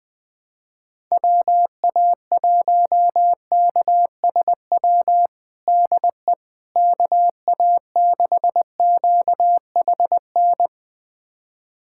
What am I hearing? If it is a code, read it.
Morse code, heard as a keyed tone, about 20 wpm, 700 Hz. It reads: WA1KSW DE KA6QHN
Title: USA Callsign DE Callsign - 6 Characters - 20wpm